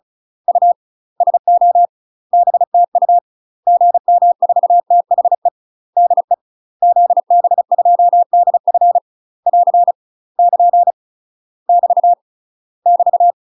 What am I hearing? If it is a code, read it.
Morse code, heard as a keyed tone, about 36 wpm, 700 Hz. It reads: U SO BTU GM4THE DE ZB2DF <AR> <KN> <BT> <BT>
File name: UK MASTER QSOs TRAINING / 35WPM / UK MASTER QSOs 0000